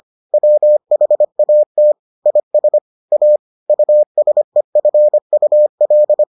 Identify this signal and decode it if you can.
Morse code, heard as a keyed tone, about 25 wpm, 600 Hz. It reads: WHAT IS A USEFUL